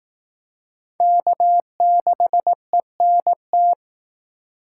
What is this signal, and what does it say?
Morse code, heard as a keyed tone, about 18 wpm, 700 Hz. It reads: K6ENT
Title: Speed Racing - USA 1x3 Call Signs 15wpm